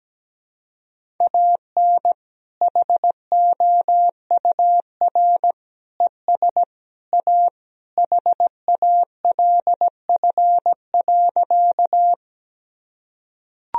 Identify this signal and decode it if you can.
Morse code, heard as a keyed tone, about 17 wpm, 700 Hz. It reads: AN HOUR ES A HALF.
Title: Sentences from Top 400 Words and Abbreviations 17wpm